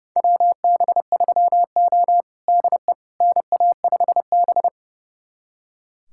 Morse code, heard as a keyed tone, about 30 wpm, 700 Hz. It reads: WB3O DE NA5B
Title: USA Callsign DE Callsign - 4 Characters - 30wpm